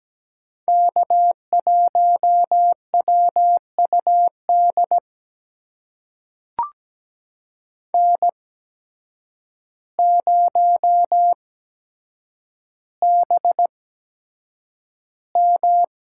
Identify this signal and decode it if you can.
Morse code, heard as a keyed tone, about 17 wpm, spaced as for 8 wpm, 700 Hz. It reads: K1WUD N0BM
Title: USA Call Signs - 8x Character Spacing - 17wpm